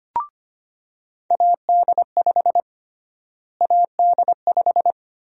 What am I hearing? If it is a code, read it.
Morse code, heard as a keyed tone, about 25 wpm, 700 Hz. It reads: AD5 AD5
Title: USA Call Sign Prefixes - Repeated 2x - 25wpm